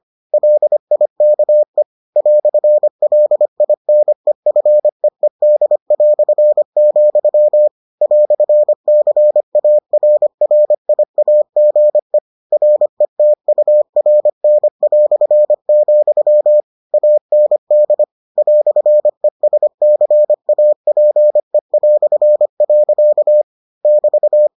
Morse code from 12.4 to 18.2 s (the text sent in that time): RETURN", AND